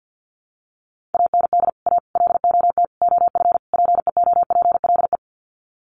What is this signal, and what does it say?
Morse code, heard as a keyed tone, about 50 wpm, 700 Hz. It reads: AND A LOT OF PEOPLE
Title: Sets of 5 Words 50wpm